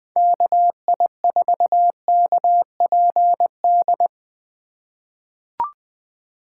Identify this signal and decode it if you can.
Morse code, heard as a keyed tone, about 20 wpm, 700 Hz. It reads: KI4KPD
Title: USA Call Signs - 6 Characters - 2x Character Spacing - 20wpm